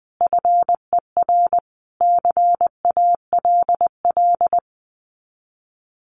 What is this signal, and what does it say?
Morse code, heard as a keyed tone, about 20 wpm, 700 Hz. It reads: FER CALL
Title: Common QSO Phrases 20wpm